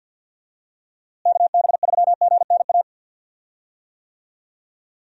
Morse code, heard as a keyed tone, about 50 wpm, 700 Hz. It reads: KB3GNA
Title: USA Call Signs - 6 Characters - Repeated 2x - 50wpm